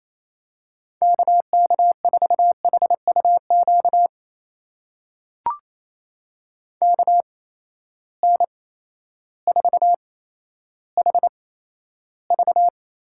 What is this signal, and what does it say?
Morse code, heard as a keyed tone, about 28 wpm, spaced as for 13 wpm, 700 Hz. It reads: KK4HUQ KN4HV
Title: USA Call Signs - 6 Characters - 8x Character Spacing - 28wpm